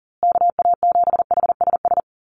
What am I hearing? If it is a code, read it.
Morse code, heard as a keyed tone, about 40 wpm, 700 Hz. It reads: KA7HSS
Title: USA Call Signs Repeated 2x 40wpm